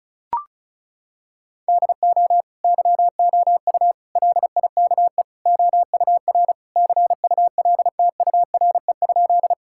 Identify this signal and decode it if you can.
Morse code, heard as a keyed tone, about 36 wpm, 700 Hz. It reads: DO YOU LIKE OUR CULTURE?
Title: Sentences from Top 1000 Words - Review Part 6 - 35wpm